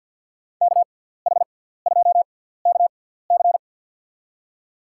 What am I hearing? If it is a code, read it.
Morse code, heard as a keyed tone, about 47 wpm, spaced as for 17 wpm, 700 Hz. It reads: KH2KC
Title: USA Call Signs - 5 Characters - 6x Character Spacing - 50wpm